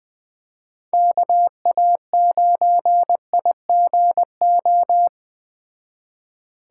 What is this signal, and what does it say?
Morse code, heard as a keyed tone, about 20 wpm, 700 Hz. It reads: KA9IGO
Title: Speed Racing - USA 2x3 Call Signs 17wpm